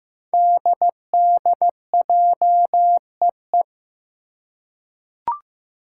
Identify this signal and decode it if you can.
Morse code, heard as a keyed tone, about 15 wpm, 700 Hz. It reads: DDJEE
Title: Five Challenging Letter-Number Combinations 15wpm